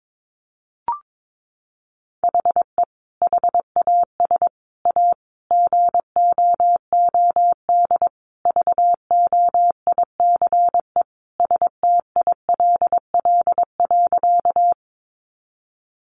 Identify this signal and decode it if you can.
Morse code, heard as a keyed tone, about 22 wpm, 700 Hz. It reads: HE HAS A GOOD VOICE STILL.
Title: Sentences from Top 500 Words 22wpm